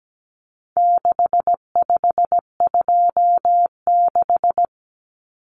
Morse code, heard as a keyed tone, about 17 wpm, 700 Hz. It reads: 6526
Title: Numbers - 4-Digit 17wpm